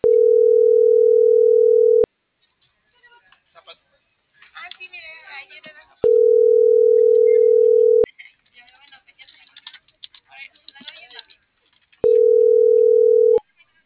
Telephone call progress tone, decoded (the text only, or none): ringback tone